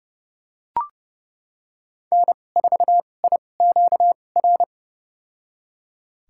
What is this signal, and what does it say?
Morse code, heard as a keyed tone, about 32 wpm, spaced as for 22 wpm, 700 Hz. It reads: N4IQR
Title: USA Call Signs - 5 Characters - 2x Character Spacing - 30wpm